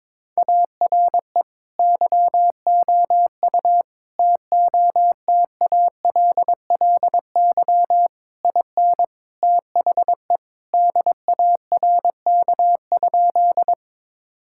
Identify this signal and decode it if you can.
Morse code, heard as a keyed tone, about 22 wpm, 700 Hz. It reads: ARE YOU TOTALLY IN THE DARK?
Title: Sentences from Top 1600 Words 22wpm